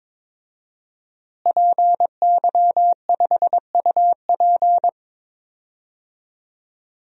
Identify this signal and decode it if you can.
Morse code, heard as a keyed tone, about 22 wpm, 700 Hz. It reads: PY5UP
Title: DXCC-100 Call Signs - 5 Characters - Repeated 3x - 22wpm